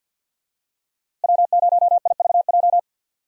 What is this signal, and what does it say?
Morse code, heard as a keyed tone, about 47 wpm, 700 Hz. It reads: W0IVJ
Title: USA Call Signs - 5 Characters - Repeated 2x - 50wpm